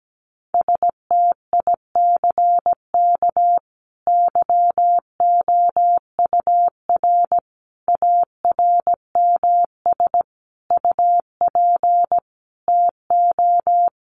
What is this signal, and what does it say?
Morse code, heard as a keyed tone, about 17 wpm, 700 Hz. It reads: STICK YOUR ARMS UP TO
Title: A Sound Of Thunder 17wpm